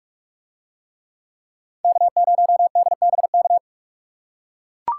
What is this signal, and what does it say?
Morse code, heard as a keyed tone, about 43 wpm, 700 Hz. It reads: K0DBK